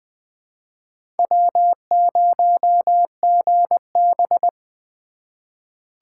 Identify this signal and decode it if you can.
Morse code, heard as a keyed tone, about 20 wpm, 700 Hz. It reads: W0GB
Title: USA Call Signs - 3x Character Spacing - 20wpm